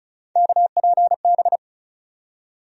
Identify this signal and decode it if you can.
Morse code, heard as a keyed tone, about 34 wpm, 700 Hz. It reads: KPB
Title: USA Call Sign Postfixes - 3 Letters 35wpm